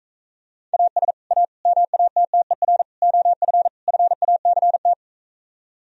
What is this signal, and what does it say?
Morse code, heard as a keyed tone, about 45 wpm, 700 Hz. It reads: AS A MATTER OF FACT